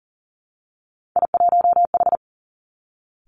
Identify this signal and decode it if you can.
Morse code, heard as a keyed tone, about 42 wpm, 700 Hz. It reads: I1H
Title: Three Challenging Letter-Number Combinations 40wpm